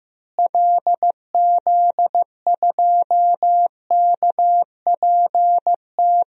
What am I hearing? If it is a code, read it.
Morse code, heard as a keyed tone, about 15 wpm, 700 Hz. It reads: LZ2KPT
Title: DXCC-100 Call Signs - 6 Characters - Repeated 3x - 15wpm